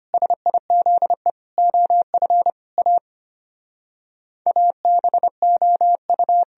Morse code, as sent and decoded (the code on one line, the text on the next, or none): ... .. --.. / --- ..-. / .- / .- -... --- ..-
SIZE OF A ABOU